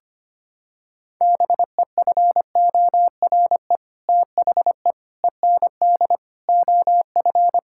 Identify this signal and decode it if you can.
Morse code, heard as a keyed tone, about 25 wpm, 700 Hz. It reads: BEFORE THE END OF
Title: Sentences from Top 1000 Words - Review Part 5 - 25wpm